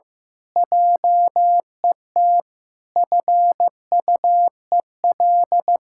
Morse code, heard as a keyed tone, about 15 wpm, 700 Hz. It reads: JET FUEL
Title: Pangram Challenge v2 - 15wpm